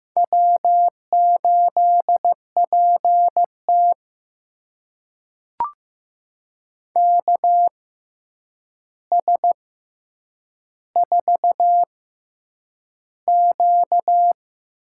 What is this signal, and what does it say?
Morse code, heard as a keyed tone, about 15 wpm, spaced as for 8 wpm, 700 Hz. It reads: W8PT KS4Q
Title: USA Call Signs - 4 Characters - 6x Character Spacing - 15wpm